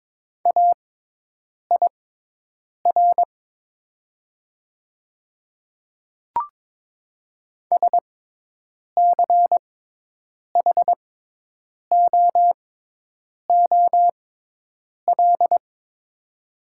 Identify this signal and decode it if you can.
Morse code, heard as a keyed tone, about 22 wpm, spaced as for 8 wpm, 700 Hz. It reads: AIR SCHOOL
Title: 6x Character Spacing - Top 300 Words - 22wpm